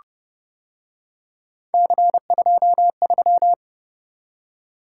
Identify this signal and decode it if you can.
Morse code, heard as a keyed tone, about 30 wpm, 700 Hz. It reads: C23